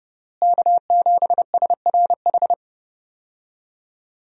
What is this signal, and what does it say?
Morse code, heard as a keyed tone, about 31 wpm, 700 Hz. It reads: K7SRH
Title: USA Call Signs - 5 Characters - Repeated 3x - 30wpm